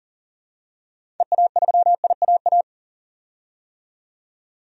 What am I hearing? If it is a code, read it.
Morse code, heard as a keyed tone, about 40 wpm, 700 Hz. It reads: EA3IAA
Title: DXCC-100 Call Signs - 6 Characters - 40wpm